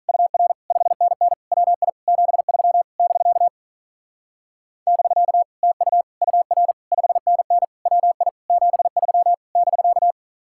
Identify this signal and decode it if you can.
Morse code, heard as a keyed tone, about 45 wpm, 700 Hz. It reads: UR 5NN WI 73 <BK> <BK> TU UR 5NN WI 73 <BK>